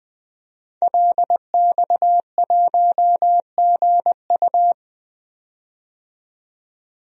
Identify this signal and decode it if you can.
Morse code, heard as a keyed tone, about 20 wpm, 700 Hz. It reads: LX1GU
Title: DXCC-100 Call Signs - 5 Characters - 20wpm